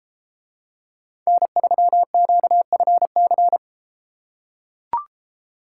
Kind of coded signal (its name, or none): Morse code